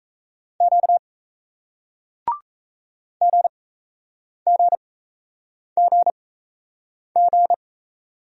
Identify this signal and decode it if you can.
Morse code, heard as a keyed tone, about 38 wpm, spaced as for 8 wpm, 700 Hz. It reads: Q GGGG